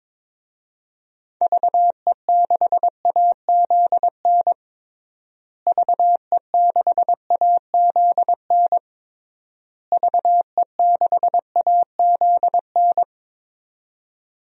Morse code, heard as a keyed tone, about 22 wpm, 700 Hz. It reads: VE6AZN VE6AZN VE6AZN
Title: DXCC-100 Call Signs - 6 Characters - Repeated 3x - 22wpm